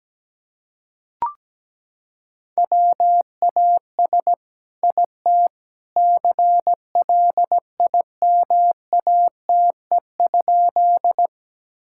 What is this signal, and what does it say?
Morse code, heard as a keyed tone, about 17 wpm, 700 Hz. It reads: WAS IT CLIMATE?
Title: Sentences from Top 1600 Words 17wpm